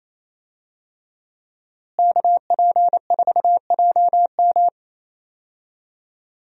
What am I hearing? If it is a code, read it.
Morse code, heard as a keyed tone, about 27 wpm, 700 Hz. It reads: KP4JM
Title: DXCC-100 Call Signs - 5 Characters - Repeated 3x - 28wpm